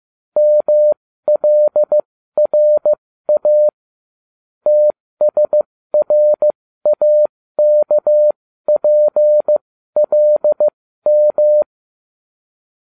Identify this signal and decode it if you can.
Morse code, heard as a keyed tone, about 15 wpm, spaced as for 13 wpm, 600 Hz. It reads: MLRA TSRAKPLM